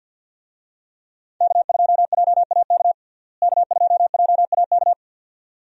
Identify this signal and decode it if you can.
Morse code, heard as a keyed tone, about 52 wpm, 700 Hz. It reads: K2JAK K2JAK
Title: USA Call Signs - 5 Characters - Repeated 2x - 50wpm